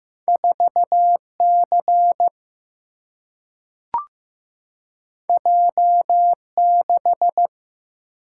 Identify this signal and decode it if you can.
Morse code, heard as a keyed tone, about 15 wpm, 700 Hz. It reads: 4 C J 6
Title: Single-Letter Single-Number 15wpm